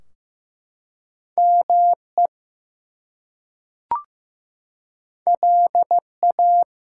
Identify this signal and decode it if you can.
Morse code, heard as a keyed tone, about 15 wpm, 700 Hz. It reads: ME LA
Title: US States and Canadian Provinces 15wpm